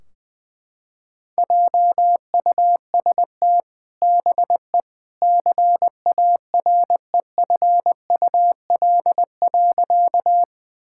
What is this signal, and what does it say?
Morse code, heard as a keyed tone, about 21 wpm, 700 Hz. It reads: JUST BE CAREFUL.
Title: Sentences from Top 1600 Words 20wpm